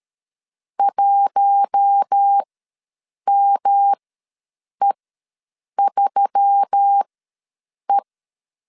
Morse code, heard as a keyed tone, about 13 wpm, spaced as for 7 wpm, 800 Hz. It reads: 1ME3E
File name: LEZIONI / 5 LBVC / 5 L ws 12wpm 2958